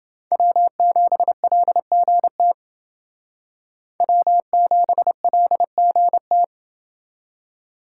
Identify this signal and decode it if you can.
Morse code, heard as a keyed tone, about 29 wpm, 700 Hz. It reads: W7LGT W7LGT